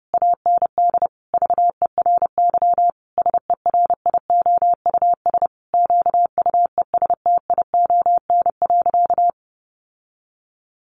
Morse code, heard as a keyed tone, about 28 wpm, 700 Hz. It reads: AND VERY SERIOUS QUESTION.
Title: Sentences from Top 1400 Words 30wpm